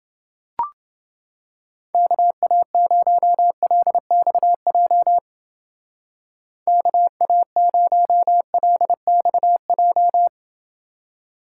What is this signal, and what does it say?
Morse code, heard as a keyed tone, about 29 wpm, 700 Hz. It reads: KA0LXJ KA0LXJ